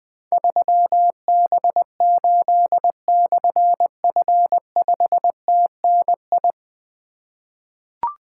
Morse code, heard as a keyed tone, about 20 wpm, 700 Hz. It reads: 3B8/F5TNI
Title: Difficult Call Signs 20wpm